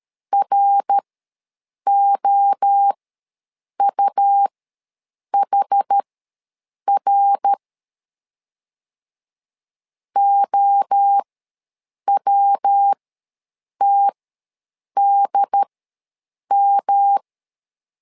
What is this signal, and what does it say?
Morse code, heard as a keyed tone, about 13 wpm, spaced as for 7 wpm, 800 Hz. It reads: ROUHR OWTDM
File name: LEZIONI / 3 MWRO / 3 MWRO ws 12wpm 1914